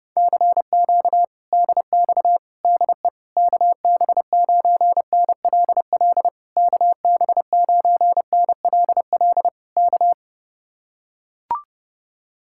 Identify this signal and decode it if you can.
Morse code, heard as a keyed tone, about 30 wpm, 700 Hz. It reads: CQ DX DE KB9NLL KB9NLL K